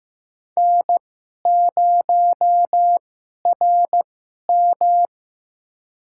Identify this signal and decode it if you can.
Morse code, heard as a keyed tone, about 15 wpm, spaced as for 11 wpm, 700 Hz. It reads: N0RM
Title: USA Call Signs - 2x Character Spacing - 15wpm